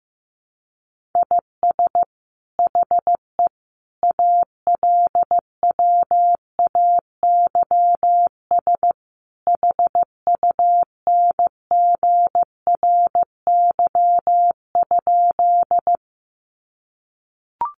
Morse code, heard as a keyed tone, about 15 wpm, 700 Hz. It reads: IS HE ALWAYS HUNGRY?
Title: Sentences from Top 1400 Words 15wpm